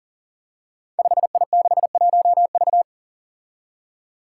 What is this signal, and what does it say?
Morse code, heard as a keyed tone, about 41 wpm, 700 Hz. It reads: 5I61V
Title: Five Challenging Letter-Number Combinations 40wpm